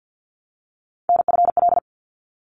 Morse code, heard as a keyed tone, about 50 wpm, 700 Hz. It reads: NFL
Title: ARRL Sections 50wpm